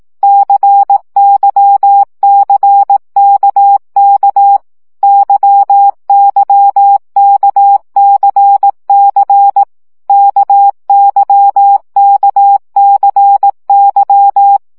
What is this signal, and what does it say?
Morse code, heard as a keyed tone, about 18 wpm, 800 Hz. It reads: CYCKK YYKCC KYKCY
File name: CW gyakorlas k c y 15 baud